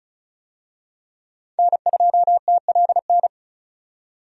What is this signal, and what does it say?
Morse code, heard as a keyed tone, about 36 wpm, 700 Hz. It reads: N2TLN
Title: USA Call Signs - 5 Characters - 35wpm